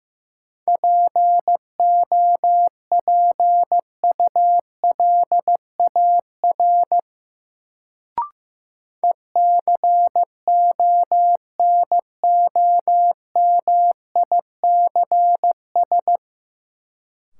Morse code, heard as a keyed tone, about 15 wpm, 700 Hz. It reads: POPULAR ECONOMICS
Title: Sentences from Top 1000 Words - Review Part 3 - New Word Drill - 15wpm